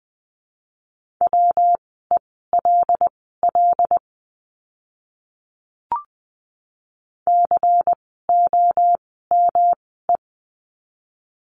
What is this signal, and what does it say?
Morse code, heard as a keyed tone, about 20 wpm, spaced as for 15 wpm, 700 Hz. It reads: WELL COME
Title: 206 Lesson 33f - Letter j - Words 20wpm